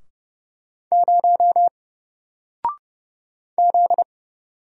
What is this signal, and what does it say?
Morse code, heard as a keyed tone, about 31 wpm, 700 Hz. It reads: 0 Z
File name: Single-Letter-Number-30wpm